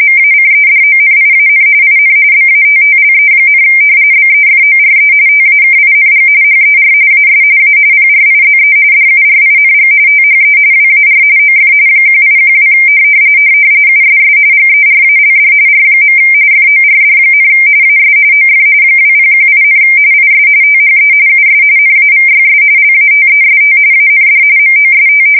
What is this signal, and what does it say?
RTTY, about 45.45 baud. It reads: MACEDONIAN IN BATTLE, BEFORE TAKING HER AS A PRIZE DURING THE WAR OF 1812 USS GERALD R. FORD, AS OF 2018, IS THE US NAVYS LATEST AND MOST
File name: fsk170 / usnavy1